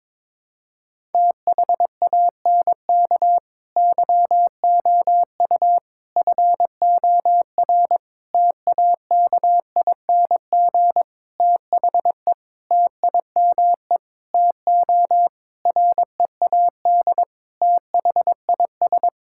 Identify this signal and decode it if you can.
Morse code, heard as a keyed tone, about 22 wpm, 700 Hz. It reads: THANK YOU FOR TAKING THE TIME TO READ THIS
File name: Sets of 9 Words - 22wpm